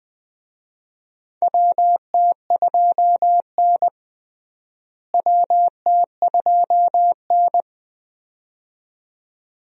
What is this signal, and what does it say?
Morse code, heard as a keyed tone, about 20 wpm, 700 Hz. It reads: WT2N WT2N
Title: USA Call Signs - 4 Characters - Repeated 2x - 20wpm